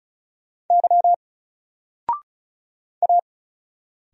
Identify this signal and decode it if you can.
Morse code, heard as a keyed tone, about 36 wpm, 700 Hz. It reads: Y A
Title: Single Letter-Number 35wpm